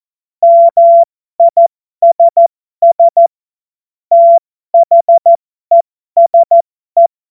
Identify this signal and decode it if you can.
Morse code, heard as a keyed tone, about 14 wpm, spaced as for 12 wpm, 680 Hz. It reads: MISS THESE
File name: Morsekurs / Witze